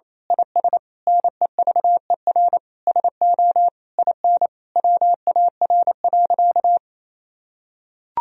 Morse code, heard as a keyed tone, about 28 wpm, 700 Hz. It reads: IS NEVER SO IN WAR.
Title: Sentences from Top 1000 Words - Review Part 2 - 28wpm